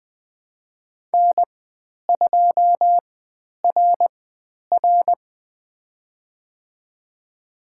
Morse code, heard as a keyed tone, about 20 wpm, spaced as for 10 wpm, 700 Hz. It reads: N2RR
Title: 095 Lesson 16g - Letter c - Call Signs 20wpm